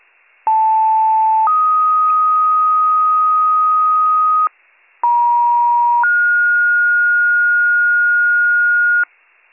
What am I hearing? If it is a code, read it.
Two-tone page, heard as a one-second tone, then a three-second tone, about 880 then 1250 Hz, about 950 then 1430 Hz, with noise. It is heard from 0.5 to 9.0 s.